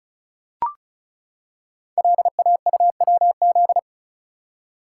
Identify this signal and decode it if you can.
Morse code, heard as a keyed tone, about 36 wpm, 700 Hz. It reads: LAUWZ
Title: Code Groups - 5 Letter:Numbers - 35wpm